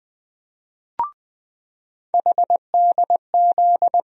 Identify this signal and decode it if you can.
Morse code, heard as a keyed tone, about 21 wpm, 700 Hz. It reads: HDZ
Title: USA Call Sign Postfixes - 3 Letters 20wpm